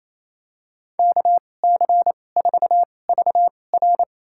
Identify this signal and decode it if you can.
Morse code, heard as a keyed tone, about 28 wpm, spaced as for 20 wpm, 700 Hz. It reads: KC4VR
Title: USA Call Signs - 5 Characters - 2x Character Spacing - 28wpm